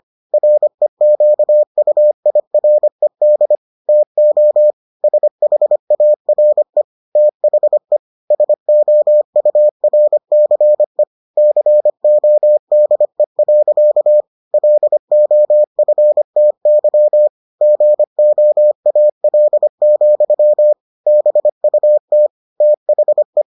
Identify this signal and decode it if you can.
Morse code, heard as a keyed tone, about 25 wpm, 600 Hz. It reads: REQUIRED TO SHARE THE SOURCE CODE. LOFTY GOAL, BUT THE